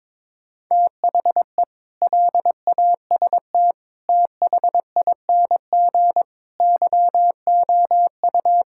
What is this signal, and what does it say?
Morse code, heard as a keyed tone, about 22 wpm, 700 Hz. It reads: THE LAST THING YOU
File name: practice / 22wpm / Sentences with 8-Word Phrases - 22wpm